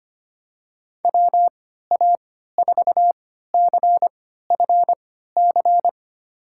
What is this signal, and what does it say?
Morse code, heard as a keyed tone, about 25 wpm, spaced as for 14 wpm, 700 Hz. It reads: WA4CFC
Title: USA Call Signs - 6 Characters - 3x Character Spacing - 25wpm